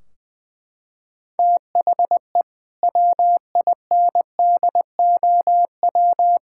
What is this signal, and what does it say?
Morse code, heard as a keyed tone, about 20 wpm, 700 Hz. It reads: THE WINDOW
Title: Sentences from Top 1600 Words 20wpm